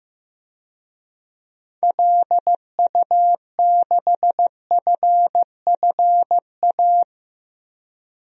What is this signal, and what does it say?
Morse code, heard as a keyed tone, about 16 wpm, 700 Hz. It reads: LU6FFA
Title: DXCC-100 Call Signs - 6 Characters - Repeated 3x - 15wpm